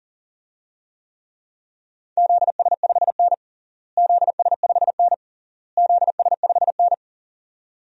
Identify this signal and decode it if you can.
Morse code, heard as a keyed tone, about 40 wpm, 700 Hz. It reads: ZS5N ZS5N ZS5N
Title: DXCC-100 Call Signs - 4 Characters - Repeated 3x - 40wpm